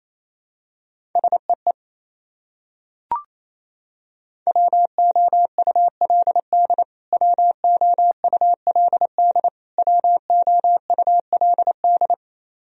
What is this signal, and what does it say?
Morse code, heard as a keyed tone, about 28 wpm, 700 Hz. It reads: SEE WOULD WOULD WOULD
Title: Top 100 Words Repeated 28wpm